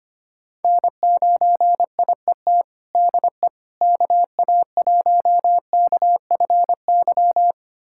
Morse code, heard as a keyed tone, about 25 wpm, 700 Hz. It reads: N9IET DE KA1KFY
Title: USA Callsign DE Callsign 25wpm